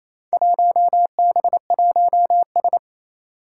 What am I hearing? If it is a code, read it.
Morse code, heard as a keyed tone, about 28 wpm, 700 Hz. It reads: 1B1S
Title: Four Challenging Letter-Number Combinations 28wpm